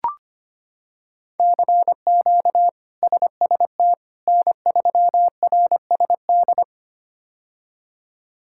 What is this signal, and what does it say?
Morse code, heard as a keyed tone, about 25 wpm, 700 Hz. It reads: CQ SST N3RSD
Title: K1USN SST QSO Practice - 25wpm